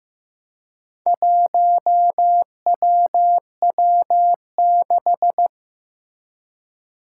Morse code, heard as a keyed tone, about 15 wpm, 700 Hz. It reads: 1WW6
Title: Four Challenging Letter-Number Combinations 15wpm